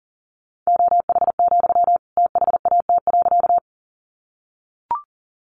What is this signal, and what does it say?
Morse code, heard as a keyed tone, about 40 wpm, 700 Hz. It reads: OH, THAT.
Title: Sentences from Top 500 Words 40wpm